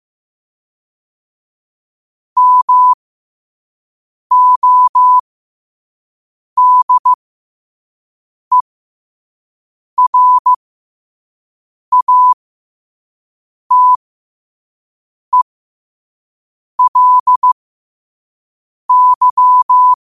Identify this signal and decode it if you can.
Morse code, heard as a keyed tone, about 15 wpm, spaced as for 5 wpm, 1000 Hz. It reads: MODERATELY
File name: qotdfiles / 5wpm